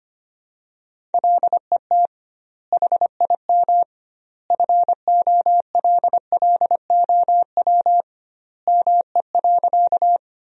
Morse code, heard as a keyed tone, about 25 wpm, 700 Hz. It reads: LET HIM FOLLOW ME.
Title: 2x Word-Spacing - Sentences from Top 300 Words - 25wpm